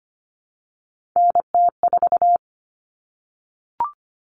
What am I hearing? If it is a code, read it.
Morse code, heard as a keyed tone, about 25 wpm, 700 Hz. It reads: NT4
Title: Common Cut Numbers - 3-Digit 25wpm